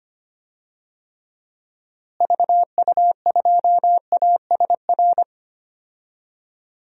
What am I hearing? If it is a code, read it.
Morse code, heard as a keyed tone, about 25 wpm, 700 Hz. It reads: VU2ASR
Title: DXCC-100 Call Signs - 6 Characters - Repeated 2x - 25wpm